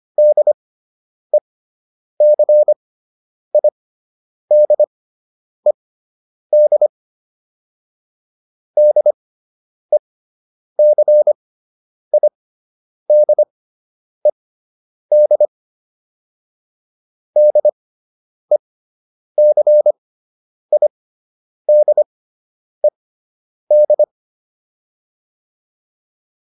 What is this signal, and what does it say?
Morse code, heard as a keyed tone, about 25 wpm, spaced as for 9 wpm, 600 Hz. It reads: DECIDED DECIDED DECIDED